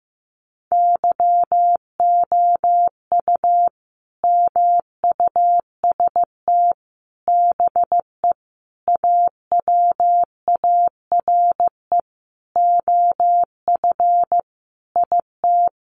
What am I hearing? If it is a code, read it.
Morse code, heard as a keyed tone, about 14 wpm, 700 Hz. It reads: YOU MUST BE AWARE OF IT
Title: Sentences from Top 1200 Words 15wpm